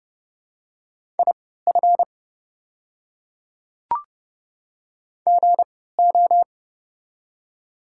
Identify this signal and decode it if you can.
Morse code, heard as a keyed tone, about 30 wpm, spaced as for 17 wpm, 700 Hz. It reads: IF GO
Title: 3x Character Spacing - Top 100 Words - 30wpm